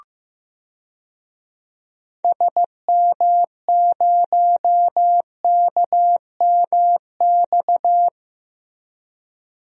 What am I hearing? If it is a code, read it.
Morse code, heard as a keyed tone, about 16 wpm, 700 Hz. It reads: SM0KMX
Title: DXCC-100 Call Signs - 6 Characters - Repeated 2x - 15wpm